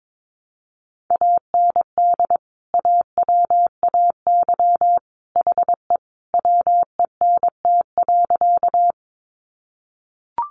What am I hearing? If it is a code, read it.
Morse code, heard as a keyed tone, about 22 wpm, 700 Hz. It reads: AND AWAY HE WENT.
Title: Sentences from Top 600 Words 22wpm